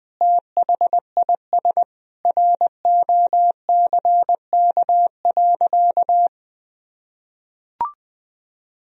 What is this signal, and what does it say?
Morse code, heard as a keyed tone, about 21 wpm, 700 Hz. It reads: THIS ROCK.
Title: Sentences from Top 1000 Words - Review Part 2 - 20wpm